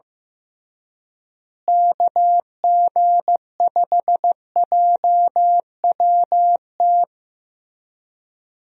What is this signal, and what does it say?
Morse code, heard as a keyed tone, about 15 wpm, 700 Hz. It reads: KG5JWT
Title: USA Call Signs - 6 Characters - Repeated 3x - 15wpm